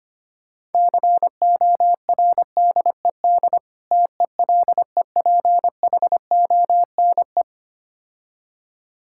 Morse code, heard as a keyed tone, about 25 wpm, 700 Hz. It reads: CORDED TELEPHONE
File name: practice / 25wpm / Difficult Words - Speed-Racing with Hints - 25wpm